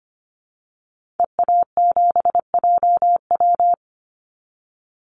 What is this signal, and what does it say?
Morse code, heard as a keyed tone, about 25 wpm, 700 Hz. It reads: EA7JW
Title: Call Signs Repeated 2x - 25wpm